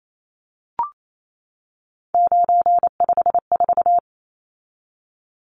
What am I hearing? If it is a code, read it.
Morse code, heard as a keyed tone, about 28 wpm, 700 Hz. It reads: 954
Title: Numbers - 3-Digit 28wpm